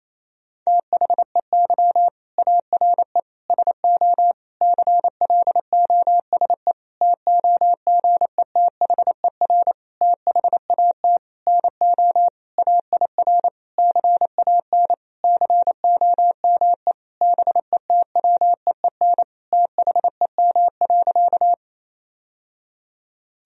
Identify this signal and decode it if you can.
Morse code, heard as a keyed tone, about 28 wpm, 700 Hz. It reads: THEY ARE SO CLOSE TOGETHER THAT NO AIR CAN COME BETWEEN THEM.
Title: Sentences from Top 1000 Words - Review Part 1 - 28wpm